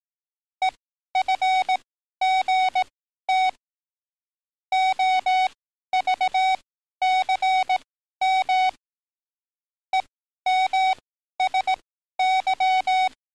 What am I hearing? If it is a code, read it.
Morse code, heard as a keyed tone, about 18 wpm, spaced as for 12 wpm, 750 Hz. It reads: EFGT OVCM EMSY
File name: training / NAVPERS 11049to11060 / NAVPERS 11053 RA RB